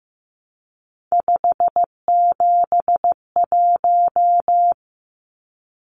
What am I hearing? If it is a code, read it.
Morse code, heard as a keyed tone, about 15 wpm, 700 Hz. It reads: 571
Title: Numbers - 3-Digit 15wpm